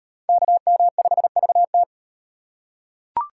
Morse code, heard as a keyed tone, about 38 wpm, 700 Hz. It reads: KM5VT